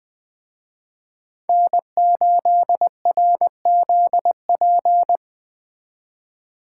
Morse code, heard as a keyed tone, about 20 wpm, 700 Hz. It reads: N8RZP
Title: Speed Racing - USA 1x3 Call Signs 17wpm